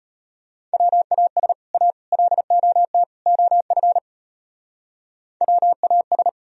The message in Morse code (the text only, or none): WAS A LOT OF WAS